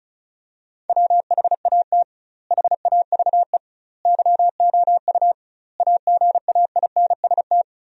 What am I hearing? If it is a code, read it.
Morse code, heard as a keyed tone, about 34 wpm, 700 Hz. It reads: WHAT HAVE YOU AGAINST
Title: 2x Word-Spacing - Sentences from Top 400 Words - 35wpm